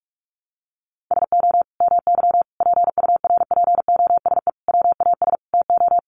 Morse code, heard as a keyed tone, about 44 wpm, 700 Hz. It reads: SO MY PURPOSE WAS TO
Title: Sentences from Top 1100 Words 45wpm